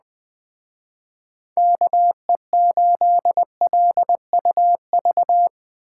Morse code, heard as a keyed tone, about 20 wpm, 700 Hz. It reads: KE8LUV
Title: Speed Racing - USA 2x3 Call Signs 17wpm